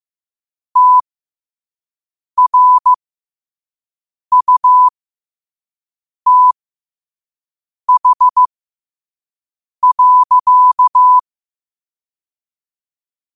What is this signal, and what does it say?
Morse code, heard as a keyed tone, about 15 wpm, spaced as for 5 wpm, 1000 Hz. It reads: TRUTH.
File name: qotdfiles / 5wpm